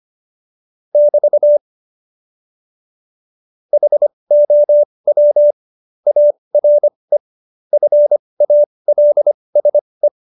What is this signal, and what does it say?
Morse code, heard as a keyed tone, about 25 wpm, spaced as for 20 wpm, 590 Hz. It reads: <BT> HOW ARE FALSE
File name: Jokes-25-char-20-eff